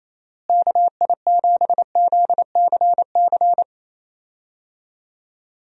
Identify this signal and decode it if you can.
Morse code, heard as a keyed tone, about 28 wpm, 700 Hz. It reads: KI7ZCC
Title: USA Call Signs - 6 Characters - Repeated 2x - 28wpm